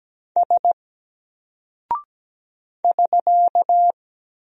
Morse code, heard as a keyed tone, about 17 wpm, 700 Hz. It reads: S <SK>
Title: Single Character (Letter:Number:Punctuation:Prosign) Alt 17wpm